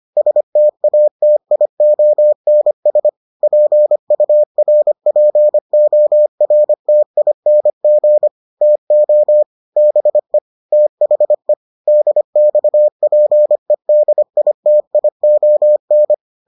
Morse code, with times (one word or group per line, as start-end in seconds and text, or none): 0.2-3.1 STATIONS
3.4-8.3 PURPORTING
8.6-9.4 TO
9.8-10.4 BE
10.7-11.5 THE
11.9-16.1 DXPEDITION